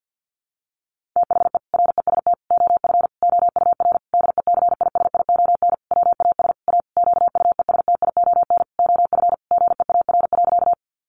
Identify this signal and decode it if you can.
Morse code, heard as a keyed tone, about 50 wpm, 700 Hz. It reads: THE REST OF OUR DECISION WAS A QUESTION OF GEAR.
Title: Sentences from Top 1400 Words 50wpm